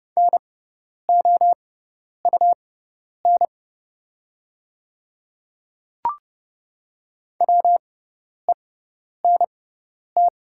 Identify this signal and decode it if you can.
Morse code, heard as a keyed tone, about 30 wpm, spaced as for 10 wpm, 700 Hz. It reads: NOUN WENT
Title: 6x Character Spacing - Top 500 Words - 30wpm